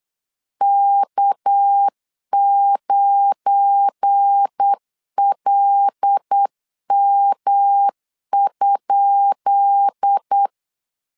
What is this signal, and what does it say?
Morse code, heard as a keyed tone, about 8 wpm, 800 Hz. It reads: K9LM?
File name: LEZIONI / 10 SIGNS / 10B INT 8wpm 3534